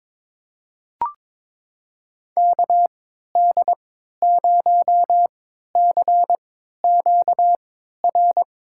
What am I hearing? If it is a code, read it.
Morse code, heard as a keyed tone, about 22 wpm, spaced as for 13 wpm, 700 Hz. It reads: KD0CQR
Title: USA Call Signs - 3x Character Spacing - 22wpm